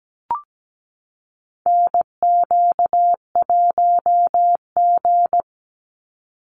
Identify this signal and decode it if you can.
Morse code, heard as a keyed tone, about 17 wpm, 700 Hz. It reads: NQ1G
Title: USA Call Signs 17wpm